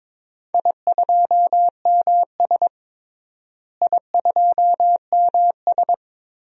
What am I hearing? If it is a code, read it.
Morse code, heard as a keyed tone, about 22 wpm, 700 Hz. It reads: I2MS I2MS
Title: DXCC-100 Call Signs - 4 Characters - Repeated 2x - 22wpm